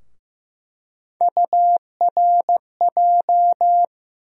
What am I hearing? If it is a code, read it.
Morse code, heard as a keyed tone, about 15 wpm, 700 Hz. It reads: URJ
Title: USA Call Sign Postfixes - 3 Letters 15wpm